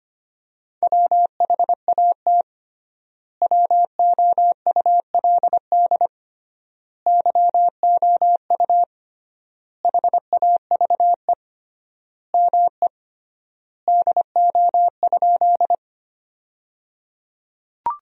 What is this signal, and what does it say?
Morse code, heard as a keyed tone, about 25 wpm, 700 Hz. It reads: WHAT WOULD YOU HAVE ME DO?
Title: 3x Word-Spacing - Sentences from Top 200 Words - 25wpm